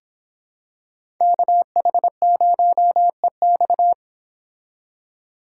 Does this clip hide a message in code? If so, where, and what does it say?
Yes: Morse code from 1.2 to 3.9 s, heard as a keyed tone, about 25 wpm, 700 Hz. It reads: KH0EX